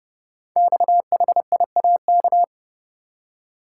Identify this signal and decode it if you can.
Morse code, heard as a keyed tone, about 30 wpm, 700 Hz. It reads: XHIAK
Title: Code Groups - 5 Letter:Numbers - 30wpm